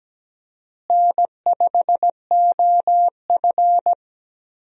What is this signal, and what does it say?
Morse code, heard as a keyed tone, about 17 wpm, 700 Hz. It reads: N5OF
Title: USA Call Signs - 4 Characters - Repeated 3x - 17wpm